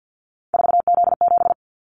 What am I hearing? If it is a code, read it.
Morse code, heard as a keyed tone, about 50 wpm, 700 Hz. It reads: 4Z7
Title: Callsign Prefixes for Top 50 DXCC - 50wpm